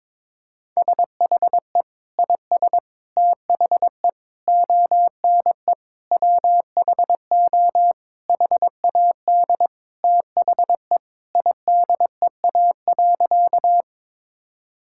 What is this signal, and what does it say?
Morse code, heard as a keyed tone, about 22 wpm, 700 Hz. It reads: SHE IS THE ONE WHO HAD THE IDEA.
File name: practice / 22wpm / Sentences with 2-Word Phrases - 22wpm